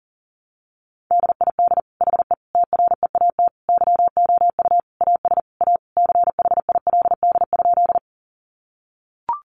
Morse code, heard as a keyed tone, about 40 wpm, 700 Hz. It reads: DID HE TREAT YOU AS A CHILD?
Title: Sentences from Top 1300 Words 40wpm